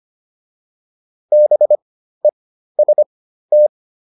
Morse code, heard as a keyed tone, about 25 wpm, spaced as for 13 wpm, 600 Hz. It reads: BEST